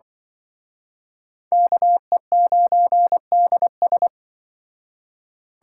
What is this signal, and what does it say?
Morse code, heard as a keyed tone, about 25 wpm, 700 Hz. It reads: KE9DS
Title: Speed Racing - USA Call Signs 20wpm